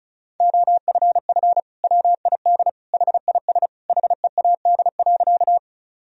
Morse code, heard as a keyed tone, about 36 wpm, 700 Hz. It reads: OFF WID HIS HEAD.
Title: Sentences from Top 300 Words and Abbreviations 35wpm